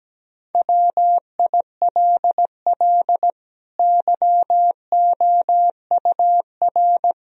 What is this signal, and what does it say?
Morse code, heard as a keyed tone, about 17 wpm, 700 Hz. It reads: WILL YOUR
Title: Sentences from Top 1000 Words - Review Part 3 - 17wpm